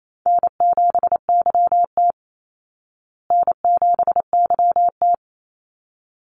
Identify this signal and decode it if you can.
Morse code, heard as a keyed tone, about 28 wpm, 700 Hz. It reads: N7YT N7YT
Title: Call Signs Repeated 3x - 28wpm